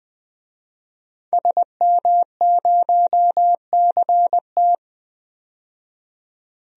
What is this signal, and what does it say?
Morse code, heard as a keyed tone, about 20 wpm, 700 Hz. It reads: SM0CT
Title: DXCC-100 Call Signs - 5 Characters - 20wpm